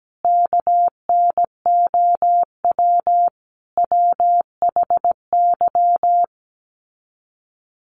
Morse code, heard as a keyed tone, about 17 wpm, 700 Hz. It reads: KNOW WHY
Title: Sets of 5 Words 17wpm